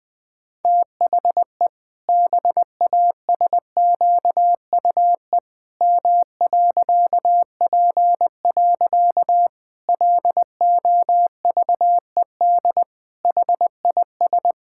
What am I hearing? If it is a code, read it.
Morse code, heard as a keyed tone, about 20 wpm, 700 Hz. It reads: THE BASQUE M.P. LOVED HIS
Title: Pangram Challenge v2 - 20wpm